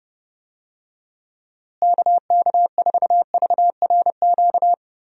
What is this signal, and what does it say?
Morse code, heard as a keyed tone, about 31 wpm, 700 Hz. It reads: KK4VRQ
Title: USA Call Signs - 6 Characters - Repeated 3x - 30wpm